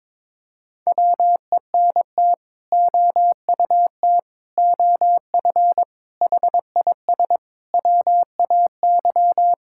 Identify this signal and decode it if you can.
Morse code, heard as a keyed tone, about 22 wpm, 700 Hz. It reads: WENT OUT OF HIS WAY
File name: practice / 22wpm / Sets of 6 Words - 22wpm